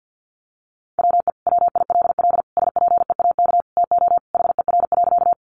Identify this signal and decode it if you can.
Morse code, heard as a keyed tone, about 50 wpm, 700 Hz. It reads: WE WILL SPEAK TO HER.
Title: Sentences from Top 1100 Words 50wpm